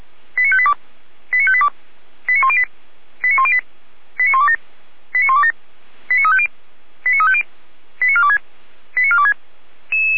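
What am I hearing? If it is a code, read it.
Five-tone selective call (ZVEI), heard as quick runs of five tones, tones of about 70 ms. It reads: 89582 89582 89108 89108 89127 89127 89350 89350 89426 89426
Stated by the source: ZVEI five-tone selective call received with an SDR.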